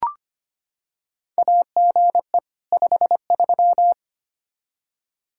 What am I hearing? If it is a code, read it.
Morse code, heard as a keyed tone, about 25 wpm, 700 Hz. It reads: AGE 53
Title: Practice QSO - Exchange 3 - More Info - 25wpm